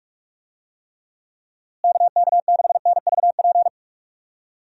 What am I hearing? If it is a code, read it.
Morse code, heard as a keyed tone, about 46 wpm, 700 Hz. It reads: KK6NVP
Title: USA Call Signs - 6 Characters - Repeated 3x - 45wpm